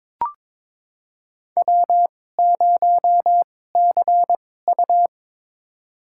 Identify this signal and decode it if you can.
Morse code, heard as a keyed tone, about 22 wpm, spaced as for 16 wpm, 700 Hz. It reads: W0CU
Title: USA Call Signs - 4 Characters - 2x Character Spacing - 22wpm